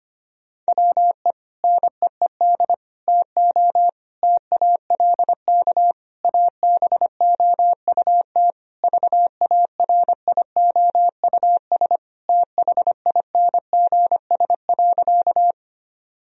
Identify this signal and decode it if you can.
Morse code, heard as a keyed tone, about 25 wpm, 700 Hz. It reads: WE NEED TO TALK ABOUT VARIOUS THINGS.
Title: Sentences from Top 1000 Words - Review Part 4 - 25wpm